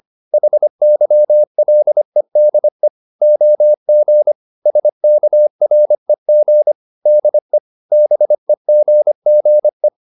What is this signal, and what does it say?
Morse code, heard as a keyed tone, about 25 wpm, 600 Hz. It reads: HYLEDE OG SKREG DE BEGGE
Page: Ferie 25 wpm